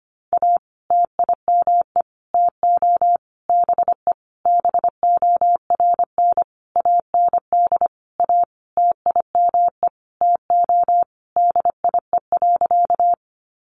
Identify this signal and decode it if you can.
Morse code, heard as a keyed tone, about 25 wpm, 700 Hz. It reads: A TIME TO BE BORN AND A TIME TO DIE.
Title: Sentences from Top 1300 Words 25wpm